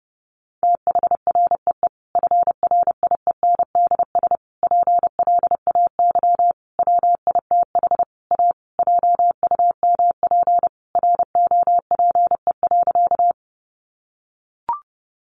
Morse code, heard as a keyed tone, about 30 wpm, 700 Hz. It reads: THREE FRIENDS PLAY WITH A JUMP ROPE.
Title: Easy Phrases 30wpm